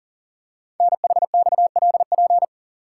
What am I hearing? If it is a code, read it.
Morse code, heard as a keyed tone, about 40 wpm, 700 Hz. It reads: NHXLP
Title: Code Groups - 5 Letter:Numbers - 40wpm